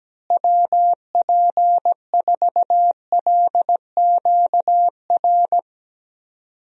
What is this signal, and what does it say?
Morse code, heard as a keyed tone, about 17 wpm, 700 Hz. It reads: WP4LQR
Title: USA Call Signs - 6 Characters - Repeated 3x - 17wpm